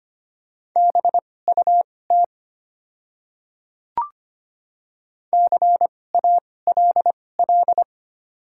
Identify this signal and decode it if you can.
Morse code, heard as a keyed tone, about 25 wpm, spaced as for 18 wpm, 690 Hz. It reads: BUT CALL
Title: 2x Character Spacing - Top 100 Words - 25wpm